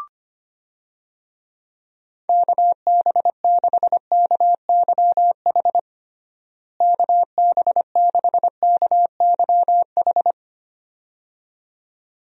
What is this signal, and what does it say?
Morse code, heard as a keyed tone, about 25 wpm, 700 Hz. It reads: KB6KYH KB6KYH
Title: USA Call Signs - 6 Characters - Repeated 2x - 25wpm